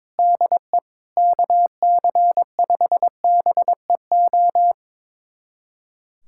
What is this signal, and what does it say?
Morse code, heard as a keyed tone, about 22 wpm, 700 Hz. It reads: DE KC5BEO
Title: USA Callsign DE Callsign - 6 Characters - 22wpm